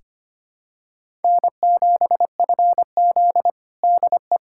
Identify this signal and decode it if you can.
Morse code, heard as a keyed tone, about 25 wpm, 700 Hz. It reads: N7FZ DE
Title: USA Callsign DE Callsign - 4 Characters - 25wpm